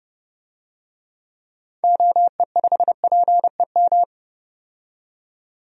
Morse code, heard as a keyed tone, about 30 wpm, 700 Hz. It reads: OE5PEM
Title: DXCC-100 Call Signs - 6 Characters - 30wpm